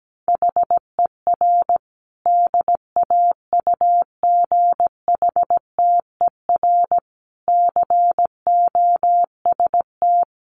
Morse code, heard as a keyed tone, about 17 wpm, 700 Hz. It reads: HER DAUGHTER COST
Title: Sentences from Top 1500 Words 17wpm